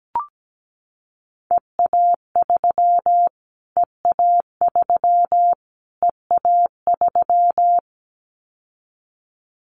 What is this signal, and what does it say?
Morse code, heard as a keyed tone, about 17 wpm, 700 Hz. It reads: EA3 EA3 EA3
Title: Callsign Prefixes for Top 25 DXCC - Repeated 3x - 17wpm